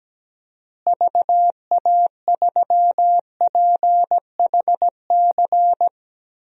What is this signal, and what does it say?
Morse code, heard as a keyed tone, about 17 wpm, 700 Hz. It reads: VA3PHC
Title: DXCC-100 Call Signs - 6 Characters - Repeated 3x - 17wpm